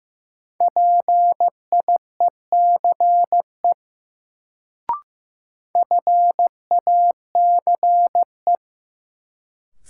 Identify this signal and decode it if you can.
Morse code, heard as a keyed tone, about 15 wpm, 700 Hz. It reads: PIECE FACE
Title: Sets of 3 Words - New Word Drill - 15wpm